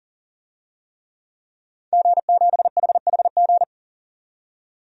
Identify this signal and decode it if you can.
Morse code, heard as a keyed tone, about 40 wpm, 700 Hz. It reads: G7HHG
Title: DXCC-100 Call Signs - 5 Characters - 40wpm